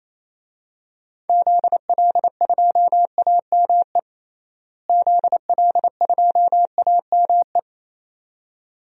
Morse code, heard as a keyed tone, about 27 wpm, 700 Hz. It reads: ZL2AME ZL2AME
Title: DXCC-100 Call Signs - 6 Characters - Repeated 2x - 28wpm